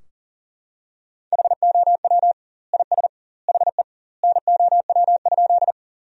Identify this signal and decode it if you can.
Morse code, heard as a keyed tone, about 40 wpm, 700 Hz. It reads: HOW IS HE NOW?
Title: 2x Word-Spacing - Sentences from Top 100 Words - 40wpm